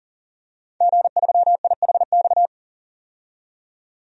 Morse code, heard as a keyed tone, about 40 wpm, 700 Hz. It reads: G3IHX
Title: DXCC-100 Call Signs - 5 Characters - 40wpm